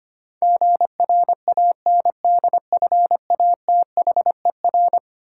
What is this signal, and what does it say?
Morse code, heard as a keyed tone, about 25 wpm, 700 Hz. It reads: GRANDFATHER
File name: practice / 25wpm / Eleven Letter Words - 25wpm